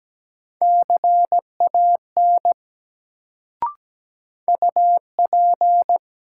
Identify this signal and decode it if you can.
Morse code, heard as a keyed tone, about 17 wpm, 700 Hz. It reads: CAN UP
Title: Sentences from Top 100 Words - New Word Drill - 17wpm